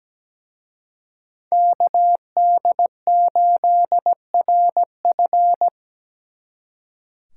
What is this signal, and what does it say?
Morse code, heard as a keyed tone, about 20 wpm, 700 Hz. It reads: KD8RF
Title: Speed Racing - USA 2x2 Call Signs 17wpm